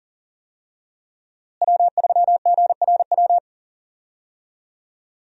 Morse code, heard as a keyed tone, about 40 wpm, 700 Hz. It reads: W3GRW
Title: USA Call Signs - 5 Characters - Repeated 2x - 40wpm